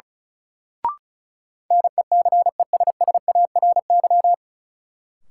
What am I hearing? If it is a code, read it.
Morse code, heard as a keyed tone, about 36 wpm, 700 Hz. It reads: NECESSARY
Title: Sentences from Top 800 Words - New Word Drill - 35wpm